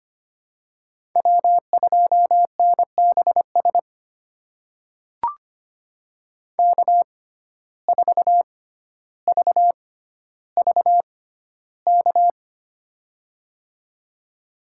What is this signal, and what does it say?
Morse code, heard as a keyed tone, about 25 wpm, spaced as for 13 wpm, 700 Hz. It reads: W2NBS K4VVK
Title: USA Call Signs - 5 Characters - 6x Character Spacing - 25wpm